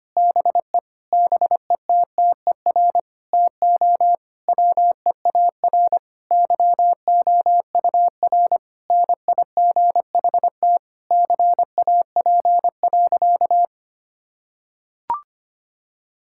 Morse code, heard as a keyed tone, about 25 wpm, 700 Hz. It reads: BE BETTER TO WEAR YOUR NIGHT CAP.
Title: Sentences from Top 1600 Words 25wpm